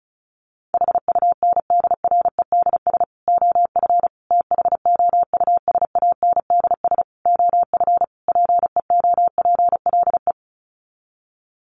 Morse code, heard as a keyed tone, about 34 wpm, 700 Hz. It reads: HUNDREDS OF THOUSANDS OF PEOPLE
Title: Sets of 5 Words Encore2 35wpm